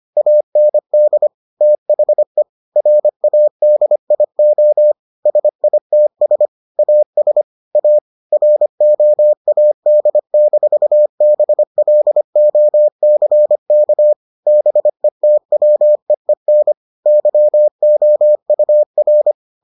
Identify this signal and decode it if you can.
Morse code, heard as a keyed tone, about 25 wpm, 600 Hz. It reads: AND THE RADIO SITS AS A ROAD-BLOCK BETWEEN YOUR